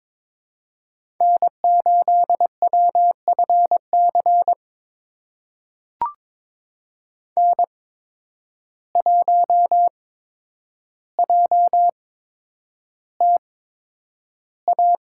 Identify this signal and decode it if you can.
Morse code, heard as a keyed tone, about 22 wpm, spaced as for 9 wpm, 700 Hz. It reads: N8WFC N1JTA
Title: USA Call Signs - 5 Characters - 8x Character Spacing - 22wpm